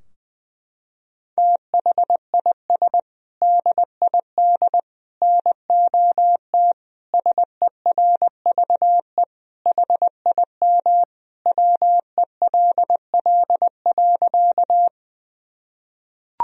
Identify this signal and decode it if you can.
Morse code, heard as a keyed tone, about 20 wpm, 700 Hz. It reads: THIS DID NOT SERVE HIM WELL.
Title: Sentences from Top 1000 Words - Review Part 2 - 20wpm